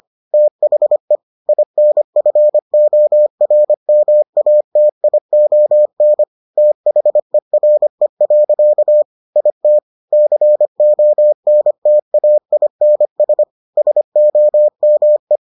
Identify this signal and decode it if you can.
Morse code, heard as a keyed tone, about 25 wpm, 600 Hz. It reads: THE INFORMATION THERE. IT CONTAINS SOME